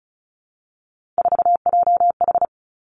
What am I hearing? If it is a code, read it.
Morse code, heard as a keyed tone, about 36 wpm, 700 Hz. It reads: 4JH
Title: Three Challenging Letter-Number Combinations 35wpm